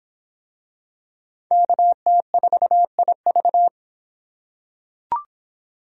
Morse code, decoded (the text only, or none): KT4IV